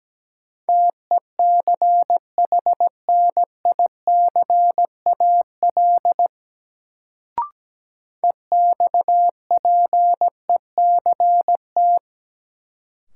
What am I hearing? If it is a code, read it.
Morse code, heard as a keyed tone, about 17 wpm, 700 Hz. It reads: TECHNICAL EXPECT
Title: Sentences from Top 1100 Words - New Word Drill - 17wpm